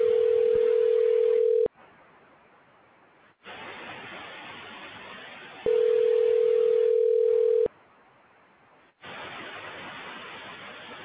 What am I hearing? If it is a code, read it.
Ringback tone (North America), 440 + 480 Hz.